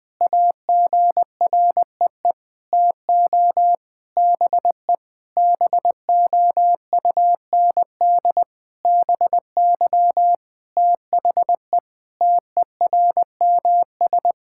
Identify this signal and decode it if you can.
Morse code, heard as a keyed tone, about 21 wpm, 700 Hz. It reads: AGREE TO BE BOUND BY THE TERMS
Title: Sentences with 9-Word Phrases - Spoken Beginning - 20wpm